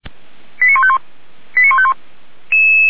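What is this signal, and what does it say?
Five-tone selective call (ZVEI), heard as quick runs of five tones, tones of about 70 ms. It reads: 89151 89151
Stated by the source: ZVEI five-tone call 89185 received by SDR.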